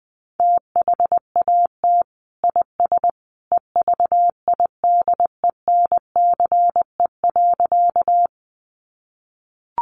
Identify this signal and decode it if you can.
Morse code, heard as a keyed tone, about 20 wpm, 700 Hz. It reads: THAT IS EVIDENCE.
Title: Sentences from Top 1300 Words 20wpm